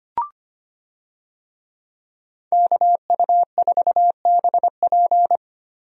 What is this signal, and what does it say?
Morse code, heard as a keyed tone, about 25 wpm, 700 Hz. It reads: KU4BP
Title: USA Call Signs - 5 Characters - Repeated 2x - 25wpm